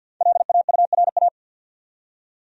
Morse code, heard as a keyed tone, about 50 wpm, 700 Hz. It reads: LAURA
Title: USA Common Names - 50wpm